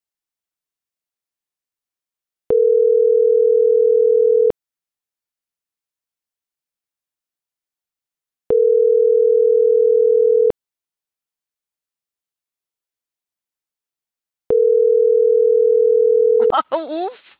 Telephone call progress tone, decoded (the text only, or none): ringback tone